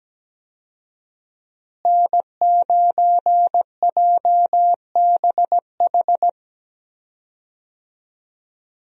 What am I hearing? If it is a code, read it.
Morse code, heard as a keyed tone, about 17 wpm, 700 Hz. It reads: N9JBH
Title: USA Call Signs - 5 Characters - Repeated 2x - 17wpm